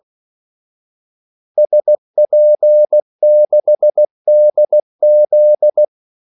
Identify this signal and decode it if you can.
Morse code, heard as a keyed tone, about 16 wpm, 600 Hz. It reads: SP6DZ
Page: Kaldesignaler 16 wpm